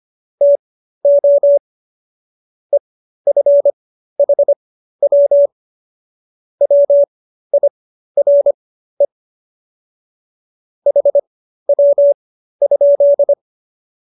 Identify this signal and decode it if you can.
Morse code, heard as a keyed tone, about 25 wpm, spaced as for 13 wpm, 570 Hz. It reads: TO EFHW WIRE HW?